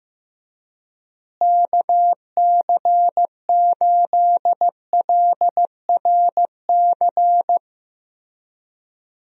Morse code, heard as a keyed tone, about 15 wpm, 700 Hz. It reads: KC8LRC
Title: USA Call Signs - 6 Characters - Repeated 3x - 15wpm